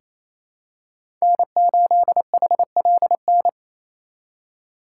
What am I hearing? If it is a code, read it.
Morse code, heard as a keyed tone, about 28 wpm, 700 Hz. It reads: N8HLN
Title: USA Call Signs - 4x Character Spacing - 28wpm